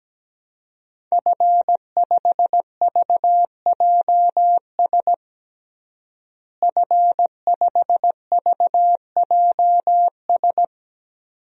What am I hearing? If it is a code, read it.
Morse code, heard as a keyed tone, about 17 wpm, 700 Hz. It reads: F5VJS F5VJS
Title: DXCC-100 Call Signs - 5 Characters - Repeated 2x - 17wpm